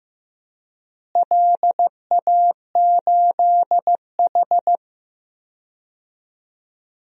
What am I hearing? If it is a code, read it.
Morse code, heard as a keyed tone, about 15 wpm, 700 Hz. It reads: LA8H
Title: DXCC-100 Call Signs - 4 Characters - Repeated 2x - 15wpm